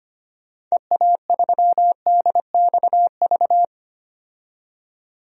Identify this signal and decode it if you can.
Morse code, heard as a keyed tone, about 25 wpm, 700 Hz. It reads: EA3DXV
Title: DXCC-100 Call Signs - 6 Characters - 25wpm